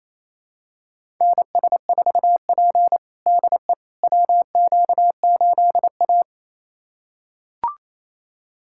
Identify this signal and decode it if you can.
Morse code, heard as a keyed tone, about 28 wpm, 700 Hz. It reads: NS4P DE WQ8A
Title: USA Callsign DE Callsign - 4 Characters - 28wpm